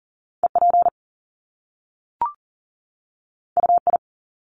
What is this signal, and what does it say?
Morse code, heard as a keyed tone, about 39 wpm, 700 Hz. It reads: EP UI
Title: Top Two Letter Combinations in Words 40wpm